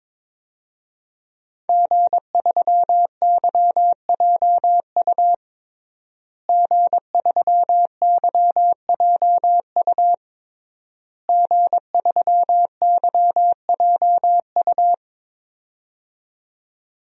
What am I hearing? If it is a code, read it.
Morse code, heard as a keyed tone, about 22 wpm, 700 Hz. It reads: G3YJU G3YJU G3YJU
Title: DXCC-100 Call Signs - 5 Characters - Repeated 3x - 22wpm